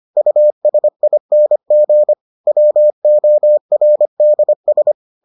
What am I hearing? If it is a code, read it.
Morse code, heard as a keyed tone, about 25 wpm, 600 Hz. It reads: USING WORDS